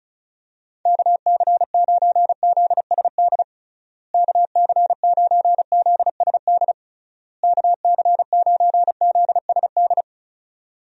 Morse code, heard as a keyed tone, about 36 wpm, 700 Hz. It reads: KC9ZSD KC9ZSD KC9ZSD
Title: USA Call Signs - 6 Characters - Repeated 3x - 35wpm